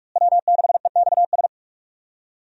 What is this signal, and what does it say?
Morse code, heard as a keyed tone, about 46 wpm, 700 Hz. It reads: W6EXS
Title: USA Call Signs - 5 Characters - Repeated 3x - 45wpm